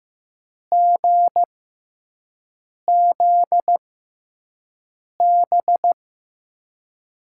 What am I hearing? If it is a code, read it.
Morse code, heard as a keyed tone, about 15 wpm, spaced as for 5 wpm, 700 Hz. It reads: GZB
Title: USA Call Signs - 6 Characters - 6x Character Spacing - 15wpm